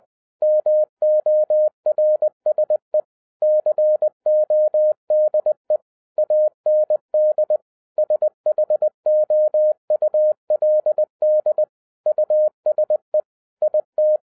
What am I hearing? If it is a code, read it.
Morse code, heard as a keyed tone, about 20 wpm, 610 Hz. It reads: MORSE CODE AND SHOULD USE IT